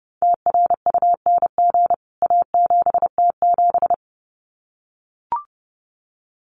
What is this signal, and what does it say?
Morse code, heard as a keyed tone, about 30 wpm, 700 Hz. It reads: TRUNG A7T7
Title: CWT Exchange 30wpm